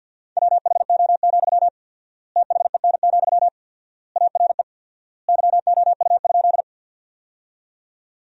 Morse code, heard as a keyed tone, about 50 wpm, 700 Hz. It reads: WHO, THEN, ARE YOU?
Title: 4x Word-Spacing - Sentences from Top 100 Words - 50wpm